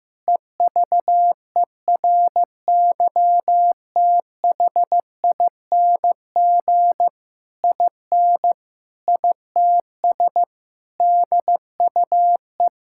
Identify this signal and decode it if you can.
Morse code, heard as a keyed tone, about 16 wpm, 700 Hz. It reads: EVERYTHING IN ITS DUE
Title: Sentences from Top 1000 Words - Review Part 6 - 15wpm